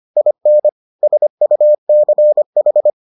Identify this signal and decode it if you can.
Morse code, heard as a keyed tone, about 25 wpm, 600 Hz. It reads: IN SUCH